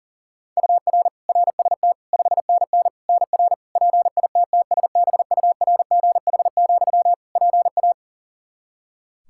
Morse code, heard as a keyed tone, about 40 wpm, 700 Hz. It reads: UR RST 5NN NR PITTSBURGH, PA
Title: Practice QSO - Exchange 1 - Beginning - 40wpm